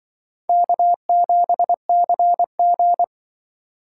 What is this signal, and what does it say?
Morse code, heard as a keyed tone, about 24 wpm, 700 Hz. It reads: K7CG
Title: Speed Racing - USA 1x2 Call Signs 20wpm